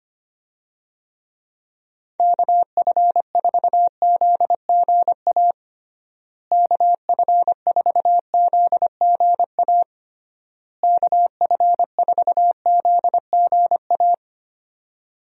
Morse code, heard as a keyed tone, about 25 wpm, 700 Hz. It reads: KF4ZGA KF4ZGA KF4ZGA
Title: USA Call Signs - 6 Characters - Repeated 3x - 25wpm